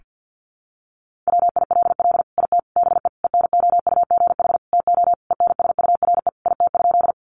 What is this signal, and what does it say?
Morse code, heard as a keyed tone, about 48 wpm, 700 Hz. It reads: WILL IT BE ENOUGH TO ENSURE IT?
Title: Sentences from Top 1100 Words 50wpm